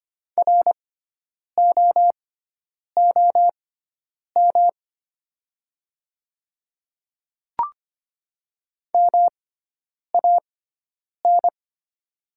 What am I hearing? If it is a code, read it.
Morse code, heard as a keyed tone, about 25 wpm, spaced as for 9 wpm, 690 Hz. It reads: ROOM MAN
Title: 6x Character Spacing - Top 300 Words - 25wpm